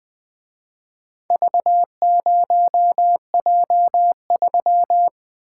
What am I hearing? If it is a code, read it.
Morse code, heard as a keyed tone, about 22 wpm, 700 Hz. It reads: V0J3
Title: Speed-Racing - Four Letter-Number Combinations - 20wpm